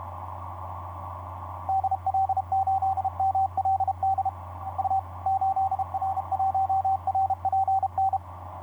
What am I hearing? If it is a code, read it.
Morse code, heard as a keyed tone, about 33 wpm, 770 Hz, with radio noise. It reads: DL8MLD U 8R1RPN
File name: 8r1rpn dl8mld 15m cw etc 27oct2010 1127